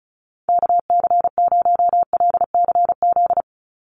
Morse code, heard as a keyed tone, about 34 wpm, 700 Hz. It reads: KC0LCZ
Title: USA Call Signs 35wpm